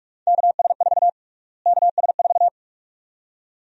Morse code, heard as a keyed tone, about 46 wpm, 700 Hz. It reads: KS4 KS4
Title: USA Call Sign Prefixes - Repeated 2x - 45wpm